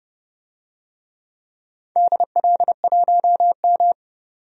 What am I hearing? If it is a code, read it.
Morse code, heard as a keyed tone, about 30 wpm, 700 Hz. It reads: DL1M
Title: DXCC-100 Call Signs - 4 Characters - 30wpm